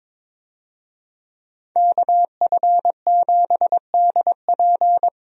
Morse code, heard as a keyed tone, about 22 wpm, 700 Hz. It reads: KF7DP
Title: USA Call Signs - 5 Characters - Repeated 2x - 22wpm